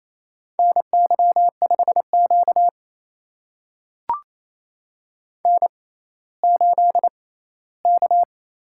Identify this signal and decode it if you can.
Morse code, heard as a keyed tone, about 28 wpm, spaced as for 16 wpm, 700 Hz. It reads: NY5Q N8K
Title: USA Call Signs - 4 Characters - 6x Character Spacing - 28wpm